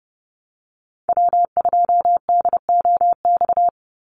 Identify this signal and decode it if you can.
Morse code, heard as a keyed tone, about 30 wpm, 700 Hz. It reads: W2DOX
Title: USA Call Signs 30wpm